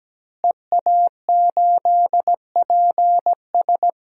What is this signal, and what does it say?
Morse code, heard as a keyed tone, about 17 wpm, 700 Hz. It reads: EA8PS
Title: DXCC-100 Call Signs - 5 Characters - Repeated 3x - 17wpm